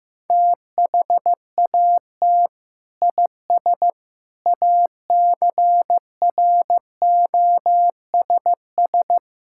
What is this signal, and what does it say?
Morse code, heard as a keyed tone, about 15 wpm, 700 Hz. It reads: THAT IS ACROSS
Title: Sentences from Top 1000 Words - Review Part 5 - 15wpm